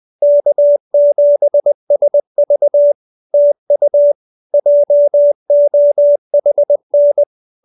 Morse code, heard as a keyed tone, about 20 wpm, 580 Hz. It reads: K7SV TU JOHN
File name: CWT-201-20